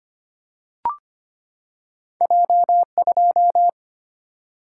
Morse code, heard as a keyed tone, about 25 wpm, 700 Hz. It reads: J 2
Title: Single-Letter Single-Number 25wpm